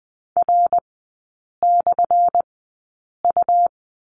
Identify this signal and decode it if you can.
Morse code, heard as a keyed tone, about 20 wpm, spaced as for 8 wpm, 700 Hz. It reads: R/U
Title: 184 Lesson 30e - Letter b - Three Characters 20wpm